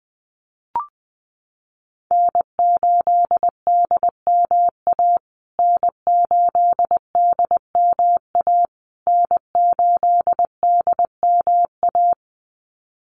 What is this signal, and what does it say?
Morse code, heard as a keyed tone, about 20 wpm, 700 Hz. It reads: N8DMA N8DMA N8DMA
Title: USA Call Signs Repeated 3x 20wpm